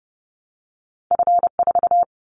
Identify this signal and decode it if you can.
Morse code, heard as a keyed tone, about 32 wpm, 700 Hz. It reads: F4
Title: Callsign Prefixes for Top 50 DXCC - 30wpm